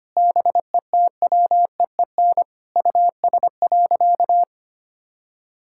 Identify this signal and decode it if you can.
Morse code, heard as a keyed tone, about 25 wpm, 700 Hz. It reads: BETWEEN US.
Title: Sentences from Top 1600 Words 25wpm